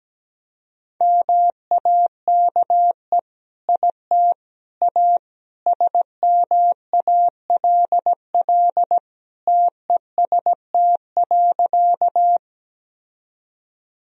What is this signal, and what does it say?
Morse code, heard as a keyed tone, about 17 wpm, 700 Hz. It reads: MAKE IT A SMALL TEST.
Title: Sentences from Top 1000 Words - Review Part 3 - 17wpm